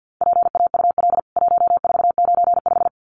Morse code, heard as a keyed tone, about 50 wpm, 700 Hz. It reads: PAUL 1495
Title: CWT Exchange 50wpm